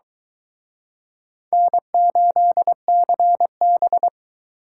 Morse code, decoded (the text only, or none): N8CB